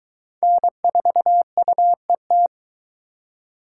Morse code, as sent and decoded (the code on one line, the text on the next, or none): -. ....- ..- . -
N4UET